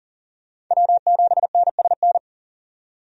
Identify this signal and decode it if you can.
Morse code, heard as a keyed tone, about 40 wpm, 700 Hz. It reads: W7NSN
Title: USA Call Signs - 5 Characters - 40wpm